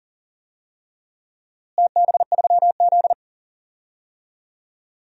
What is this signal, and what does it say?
Morse code, heard as a keyed tone, about 37 wpm, 700 Hz. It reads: TB3Z
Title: DXCC-100 Call Signs - 4 Characters - Repeated 3x - 40wpm